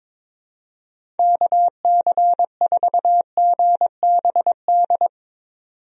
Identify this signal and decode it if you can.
Morse code, heard as a keyed tone, about 22 wpm, 700 Hz. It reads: KC4GBD
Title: USA Call Signs - 6 Characters - 22wpm